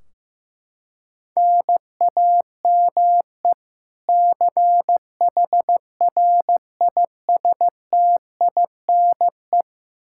Morse code, heard as a keyed tone, about 16 wpm, 700 Hz. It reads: NAME CHRISTINE
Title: Practice QSO - Exchange 1 - Beginning - 15wpm